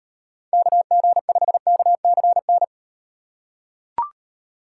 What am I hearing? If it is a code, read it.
Morse code, heard as a keyed tone, about 38 wpm, 700 Hz. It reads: KG5KCN